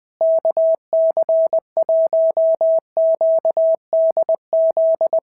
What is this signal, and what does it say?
Morse code, heard as a keyed tone, about 19 wpm, 650 Hz. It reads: KC1QDZ
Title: POTA QSO Practice - 20wpm